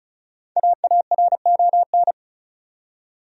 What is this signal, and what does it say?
Morse code, heard as a keyed tone, about 34 wpm, 700 Hz. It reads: AARON
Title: USA Common Names - Max 5 Letters - 35wpm